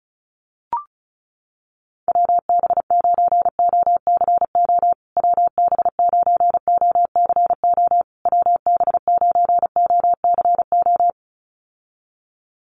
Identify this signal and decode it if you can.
Morse code, heard as a keyed tone, about 34 wpm, 700 Hz. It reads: WB9OCO WB9OCO WB9OCO
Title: USA Call Signs Repeated 3x 35wpm